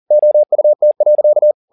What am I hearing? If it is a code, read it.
Morse code, heard as a keyed tone, about 40 wpm, 600 Hz. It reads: OUT.
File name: the old man and the sea 2 40wpm